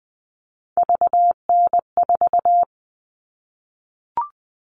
Morse code, heard as a keyed tone, about 20 wpm, 700 Hz. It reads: VN4
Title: All Cut Numbers - 3-Digit 20wpm